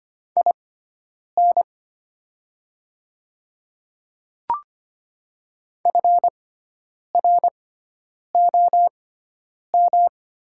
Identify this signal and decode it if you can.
Morse code, heard as a keyed tone, about 25 wpm, spaced as for 9 wpm, 700 Hz. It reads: IN FROM
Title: 6x Character Spacing - Top 100 Words - 25wpm